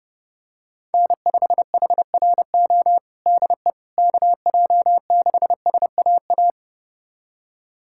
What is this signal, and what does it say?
Morse code, heard as a keyed tone, about 30 wpm, 700 Hz. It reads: N5HRO DE KJ6SAA
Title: USA Callsign DE Callsign 30wpm